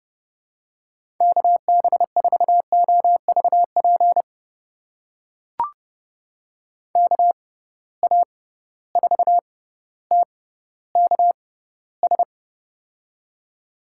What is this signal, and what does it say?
Morse code, heard as a keyed tone, about 30 wpm, spaced as for 15 wpm, 700 Hz. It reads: KB4OVP KA4TKS
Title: USA Call Signs - 6 Characters - 6x Character Spacing - 30wpm